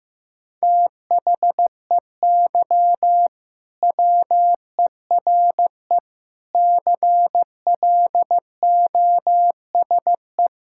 Morse code, heard as a keyed tone, about 15 wpm, 700 Hz. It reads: THEY WERE CLOSE
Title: Sentences from Top 1000 Words - Review Part 6 - 15wpm